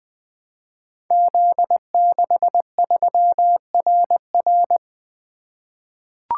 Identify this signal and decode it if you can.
Morse code, heard as a keyed tone, about 20 wpm, 700 Hz. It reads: Z63RR
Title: Code Groups - 5 Letter:Numbers - 20wpm